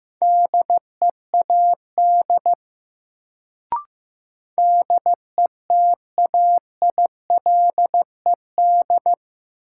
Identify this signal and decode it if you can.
Morse code, heard as a keyed tone, about 15 wpm, 700 Hz. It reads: DEAD DETAILED
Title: Sentences from Top 1500 Words - New Word Drill Alt - 15wpm